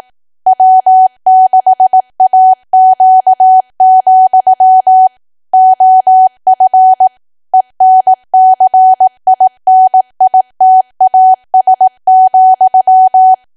Morse code, heard as a keyed tone, about 18 wpm, 750 Hz. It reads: W6AQ, OF ENCINITAS,
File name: Archive / 18 WPM / 18WPM